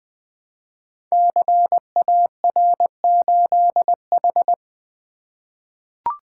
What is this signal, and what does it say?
Morse code, heard as a keyed tone, about 20 wpm, 700 Hz. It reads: CAR8H
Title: Code Groups - 5 Letter:Numbers - 20wpm